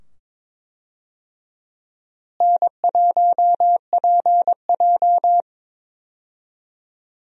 Morse code, heard as a keyed tone, about 21 wpm, 700 Hz. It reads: N1PJ
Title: USA Call Signs - 4 Characters - Repeated 2x - 22wpm